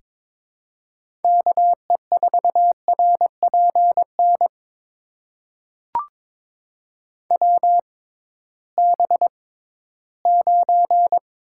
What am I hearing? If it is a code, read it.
Morse code, heard as a keyed tone, about 21 wpm, spaced as for 14 wpm, 700 Hz. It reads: KE4RPN WB9
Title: USA Call Signs - 6 Characters - 6x Character Spacing - 22wpm